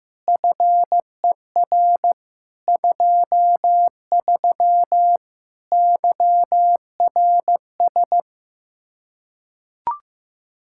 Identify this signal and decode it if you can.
Morse code, heard as a keyed tone, about 15 wpm, 700 Hz. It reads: FER 23 YRS
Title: Practice QSO - Exchange 3 - More Info - 15wpm